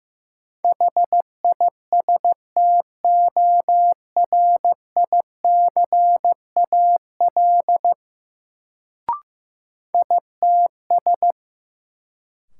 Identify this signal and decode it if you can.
Morse code, heard as a keyed tone, about 15 wpm, 700 Hz. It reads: HISTORICAL ITS
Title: Sentences from Top 600 Words - New Word Drill - 15wpm